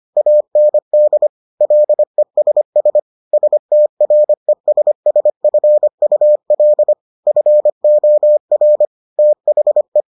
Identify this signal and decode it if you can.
Morse code, heard as a keyed tone, about 25 wpm, 600 Hz. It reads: AND LESS STRESSFUL FOR THE